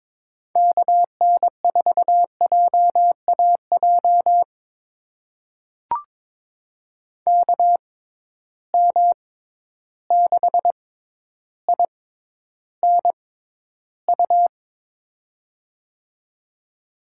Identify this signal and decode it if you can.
Morse code, heard as a keyed tone, about 22 wpm, spaced as for 11 wpm, 700 Hz. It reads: KN4JAJ KM6INU
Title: USA Call Signs - 6x Character Spacing - 22wpm